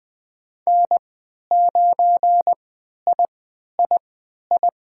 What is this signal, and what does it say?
Morse code, heard as a keyed tone, about 20 wpm, spaced as for 11 wpm, 700 Hz. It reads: N9III
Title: USA Call Signs - 3x Character Spacing - 20wpm